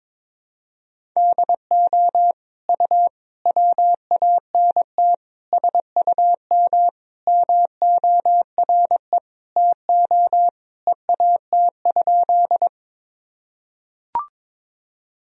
Morse code, heard as a keyed tone, about 22 wpm, 700 Hz. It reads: DO U WANT SUM MORE TO EAT?
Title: Sentences from Top 300 Words and Abbreviations 22wpm